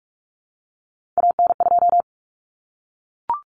Morse code, heard as a keyed tone, about 44 wpm, 700 Hz. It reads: AN2
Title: Common Cut Numbers - 3-Digit 45wpm